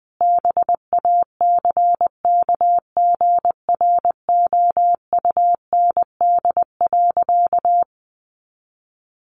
Morse code, heard as a keyed tone, about 20 wpm, 700 Hz. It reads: BACKGROUND.
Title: Sentences from Top 1500 Words 20wpm